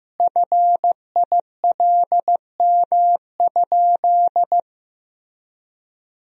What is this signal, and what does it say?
Morse code, heard as a keyed tone, about 15 wpm, 700 Hz. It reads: FILM?
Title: Sentences from Top 1000 Words - Review Part 3 - 15wpm